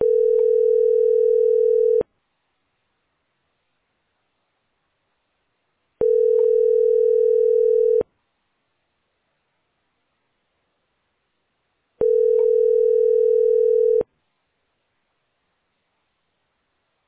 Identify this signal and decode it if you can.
Ringback tone (North America), 440 + 480 Hz.